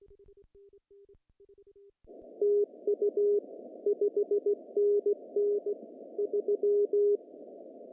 Morse code, heard as a keyed tone, about 16 wpm, 410 Hz. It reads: TU 5NN 3
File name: CW / cqww-cw-k5tr